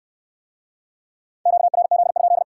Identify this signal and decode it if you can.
Morse code, heard as a keyed tone, about 70 wpm, 700 Hz. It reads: QRZ?